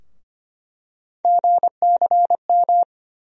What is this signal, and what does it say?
Morse code, heard as a keyed tone, about 25 wpm, 700 Hz. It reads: GCM
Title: USA Call Sign Postfixes - 3 Letters 25wpm